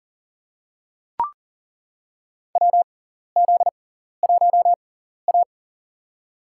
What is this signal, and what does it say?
Morse code, heard as a keyed tone, about 39 wpm, spaced as for 14 wpm, 700 Hz. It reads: WZ1A
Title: USA Call Signs - 6x Character Spacing - 40wpm